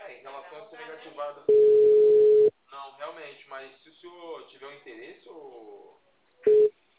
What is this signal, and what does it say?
Ringback tone (Europe, 425 Hz), with speech.